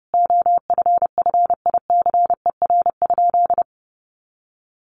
Morse code, heard as a keyed tone, about 30 wpm, 700 Hz. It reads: OFFICER?
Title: Sentences from Top 1200 Words 30wpm